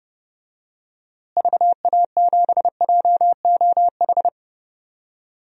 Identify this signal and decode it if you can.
Morse code, heard as a keyed tone, about 32 wpm, 700 Hz. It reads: VA7JOH
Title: DXCC-100 Call Signs - 6 Characters - Repeated 2x - 30wpm